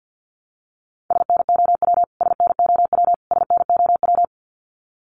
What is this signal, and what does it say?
Morse code, heard as a keyed tone, about 50 wpm, 700 Hz. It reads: SNOW SNOW SNOW
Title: Top 500 Words Repeated 50wpm